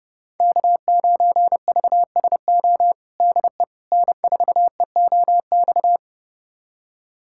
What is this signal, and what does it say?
Morse code, heard as a keyed tone, about 30 wpm, 700 Hz. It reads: K9VSO DE N4EOX
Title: USA Callsign DE Callsign - 5 Characters - 30wpm